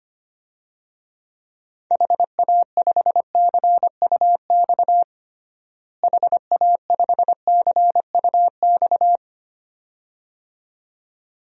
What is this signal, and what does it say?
Morse code, heard as a keyed tone, about 25 wpm, 700 Hz. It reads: HA5CUX HA5CUX
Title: DXCC-100 Call Signs - 6 Characters - Repeated 2x - 25wpm